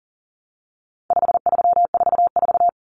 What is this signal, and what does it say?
Morse code, heard as a keyed tone, about 40 wpm, 700 Hz. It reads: 5344
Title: Numbers - 4-Digit 40wpm